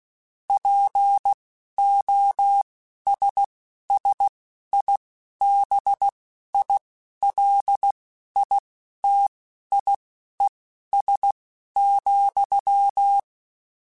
Morse code, heard as a keyed tone, about 16 wpm, spaced as for 11 wpm, 800 Hz. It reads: POSSIBILITIES,
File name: Technical Subjects / Technical Subjects-mp3s / AND Gate @10